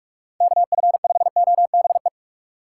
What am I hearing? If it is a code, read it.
Morse code, heard as a keyed tone, about 45 wpm, 700 Hz. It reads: KF5OBE
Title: USA Call Signs - 6 Characters - 3x Character Spacing - 45wpm